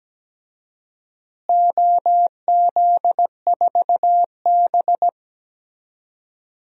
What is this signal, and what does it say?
Morse code, heard as a keyed tone, about 17 wpm, 700 Hz. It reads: OZ4B
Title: DXCC-100 Call Signs - 4 Characters - Repeated 2x - 17wpm